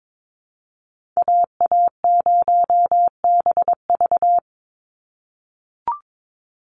Morse code, heard as a keyed tone, about 22 wpm, 700 Hz. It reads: AA0BV
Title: NA SOTA Activator Callsigns 22wpm